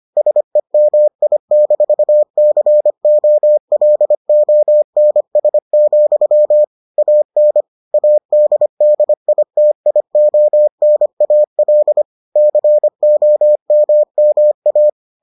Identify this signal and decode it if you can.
Morse code, heard as a keyed tone, about 25 wpm, 600 Hz. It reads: SEMI-COLONS, AN ADDITIONAL COMMA